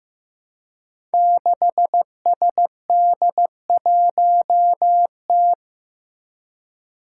Morse code, heard as a keyed tone, about 15 wpm, 700 Hz. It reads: 6SD1T
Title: Five Challenging Letter-Number Combinations 15wpm